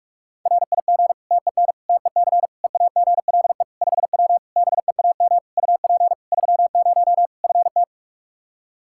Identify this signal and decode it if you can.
Morse code, heard as a keyed tone, about 46 wpm, 700 Hz. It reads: RIG TEN TEC EAGLE 5W BEAM UP 30 FT
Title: Practice QSO - Exchange 2 - Info - 45wpm